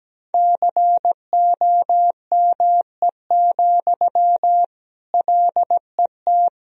Morse code, heard as a keyed tone, about 17 wpm, 700 Hz. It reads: COME, LET
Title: Sentences from Top 1000 Words - Review Part 4 - 17wpm